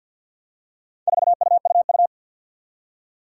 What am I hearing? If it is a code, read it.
Morse code, heard as a keyed tone, about 51 wpm, 700 Hz. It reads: 4UUU
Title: Four Challenging Letter-Number Combinations 50wpm